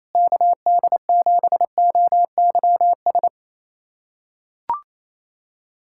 Morse code, heard as a keyed tone, about 27 wpm, 700 Hz. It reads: KD7OYS
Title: USA Callsign DE Callsign - 6 Characters - 28wpm